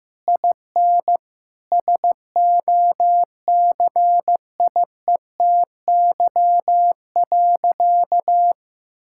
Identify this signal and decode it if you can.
Morse code, heard as a keyed tone, about 15 wpm, 700 Hz. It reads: IN SOCIETY.
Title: Sentences from Top 1600 Words 15wpm